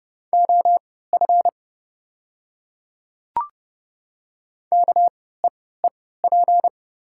Morse code, heard as a keyed tone, about 29 wpm, spaced as for 17 wpm, 690 Hz. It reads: OF KEEP
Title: 3x Character Spacing - Top 300 Words - 30wpm